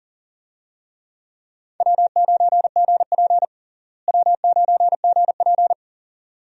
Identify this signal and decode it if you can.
Morse code, heard as a keyed tone, about 40 wpm, 700 Hz. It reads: W9GP W9GP
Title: USA Call Signs - 4 Characters - Repeated 2x - 40wpm